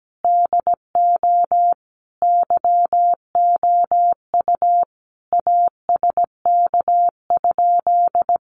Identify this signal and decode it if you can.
Morse code, heard as a keyed tone, about 17 wpm, 700 Hz. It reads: DO YOU ASK?
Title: Sentences from Top 200 Words 17wpm